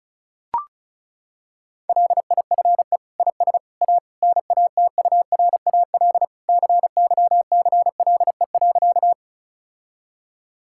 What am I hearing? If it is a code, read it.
Morse code, heard as a keyed tone, about 36 wpm, 700 Hz. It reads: LIFE IS A NATURAL CYCLE.
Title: Sentences from Top 1000 Words - Review Part 6 - 35wpm